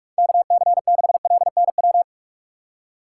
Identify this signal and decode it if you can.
Morse code, heard as a keyed tone, about 46 wpm, 700 Hz. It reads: KC6LNW
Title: USA Call Signs - 6 Characters - 6x Character Spacing - 45wpm